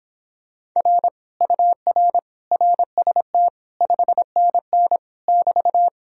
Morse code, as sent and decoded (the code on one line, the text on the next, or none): .-. / ..- .-. / .-. ... - / ..... -. -. / -...-
R UR RST 5NN <BT>